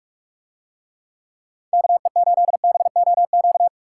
Morse code, heard as a keyed tone, about 44 wpm, 700 Hz. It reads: KE8BOQ